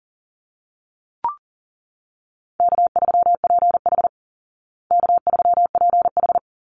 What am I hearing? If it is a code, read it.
Morse code, heard as a keyed tone, about 40 wpm, 700 Hz. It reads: K3PH K3PH
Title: Easy Call Signs Repeated 3x - 40wpm